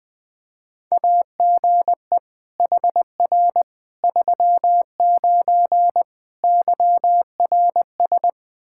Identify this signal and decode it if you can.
Morse code, heard as a keyed tone, about 20 wpm, 700 Hz. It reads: AGE HR 39 YRS
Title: Practice QSO - Exchange 3 - More Info - 20wpm